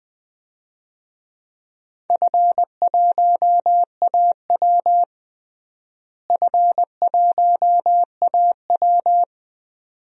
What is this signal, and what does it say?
Morse code, heard as a keyed tone, about 21 wpm, 700 Hz. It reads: F1AW F1AW
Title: DXCC-100 Call Signs - 4 Characters - Repeated 2x - 20wpm